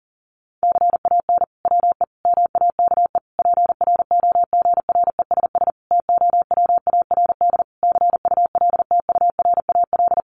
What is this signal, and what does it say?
Morse code, heard as a keyed tone, about 40 wpm, 700 Hz. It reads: CAN WE MAKE PROGRESS TOWARD CULTURAL
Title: Sentences from Top 1500 Words 40wpm